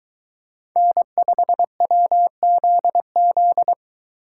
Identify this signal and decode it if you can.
Morse code, heard as a keyed tone, about 22 wpm, 700 Hz. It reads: N5WZZ